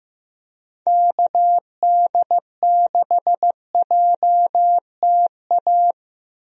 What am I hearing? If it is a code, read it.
Morse code, heard as a keyed tone, about 15 wpm, 700 Hz. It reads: KD6JTA
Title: USA Call Signs - 6 Characters - Repeated 3x - 15wpm